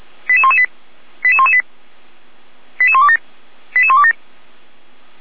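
Five-tone selective call (ZVEI), heard as quick runs of five tones, tones of about 70 ms. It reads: 88118 88118 88127 88127
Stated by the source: ZVEI five-tone call received on SDR, from audio.f85360.feuerwehr-orlamuende.org, Orlamünde fire brigade.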